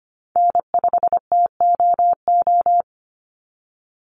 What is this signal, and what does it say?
Morse code, heard as a keyed tone, about 25 wpm, 700 Hz. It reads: N5TOO
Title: Call Signs - 25wpm